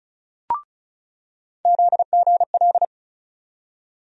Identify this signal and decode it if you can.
Morse code, heard as a keyed tone, about 36 wpm, 700 Hz. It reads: ZGL
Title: USA Call Sign Postfixes - 3 Letters 35wpm